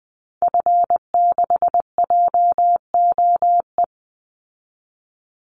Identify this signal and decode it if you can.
Morse code, heard as a keyed tone, about 20 wpm, 700 Hz. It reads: F6JOE
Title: Call Signs Repeated 3x - 20wpm